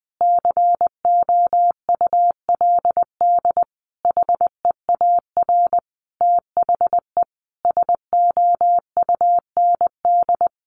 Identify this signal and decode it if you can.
Morse code, heard as a keyed tone, about 20 wpm, 700 Hz. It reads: COULD HEAR THE SOUND
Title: Sentences from Top 1100 Words 20wpm